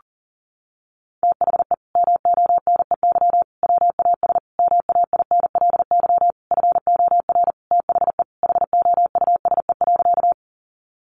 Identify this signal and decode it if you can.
Morse code, heard as a keyed tone, about 37 wpm, 700 Hz. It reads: THE MONEY WAS MAINLY FOR THE HOUSE.
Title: Sentences from Top 1400 Words 40wpm